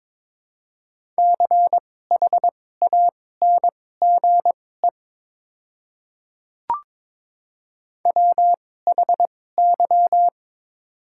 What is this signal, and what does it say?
Morse code, heard as a keyed tone, about 21 wpm, spaced as for 16 wpm, 700 Hz. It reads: CHANGE WHY
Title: 2x Character Spacing - Top 200 Words - 22wpm